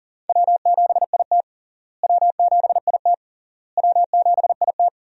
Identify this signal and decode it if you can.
Morse code, heard as a keyed tone, about 37 wpm, 700 Hz. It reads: W7IT W7IT W7IT
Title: USA Call Signs - 4 Characters - Repeated 3x - 40wpm